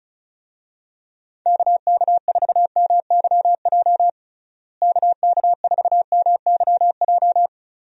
Morse code, heard as a keyed tone, about 34 wpm, 700 Hz. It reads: KK4MYJ KK4MYJ
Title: USA Call Signs - 6 Characters - Repeated 2x - 35wpm